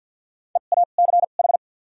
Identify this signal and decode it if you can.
Morse code, heard as a keyed tone, about 53 wpm, 700 Hz. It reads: E A C H
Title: 2x Character Spacing - Top 200 Words - 50wpm